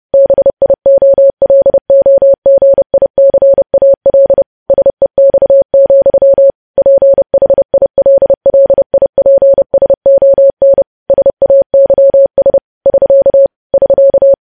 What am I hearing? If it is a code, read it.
Morse code, heard as a keyed tone, about 30 wpm, 550 Hz. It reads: BIOLOGICAL SEX, PHILLIPSON SAYS <SK> <SK>
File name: podcast / News headlines-1x-30x30